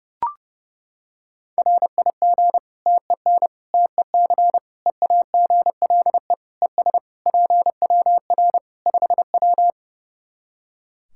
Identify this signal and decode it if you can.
Morse code, heard as a keyed tone, about 32 wpm, 700 Hz. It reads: RIG TEN TEC EAGLE ES PWR 5W
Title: Practice QSO - Exchange 2 - Info - 30wpm